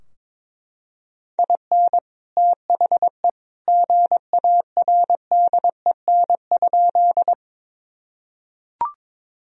Morse code, heard as a keyed tone, about 21 wpm, 700 Hz. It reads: IN THE GARDEN?
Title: Sentences from Top 1000 Words - Review Part 4 - 22wpm